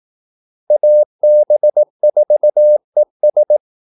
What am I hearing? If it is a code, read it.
Morse code, heard as a keyed tone, about 18 wpm, 600 Hz. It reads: AB4ES
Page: Kaldesignaler 18 wpm